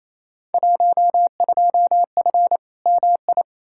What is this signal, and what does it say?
Morse code, heard as a keyed tone, about 28 wpm, 700 Hz. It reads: 12F MI
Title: Field Day Exchanges Encore - 28wpm